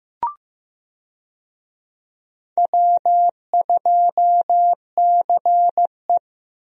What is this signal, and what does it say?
Morse code, heard as a keyed tone, about 15 wpm, 700 Hz. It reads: W2CE
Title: USA Call Signs - 4 Characters - 15wpm